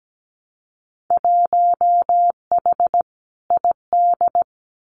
Morse code, heard as a keyed tone, about 17 wpm, 700 Hz. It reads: 1H ID
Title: Winter Field Day Exchanges 17wpm